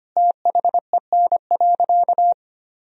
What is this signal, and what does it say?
Morse code, heard as a keyed tone, about 25 wpm, 700 Hz. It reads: THEN.
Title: Sentences from Top 200 Words and Abbreviations 25wpm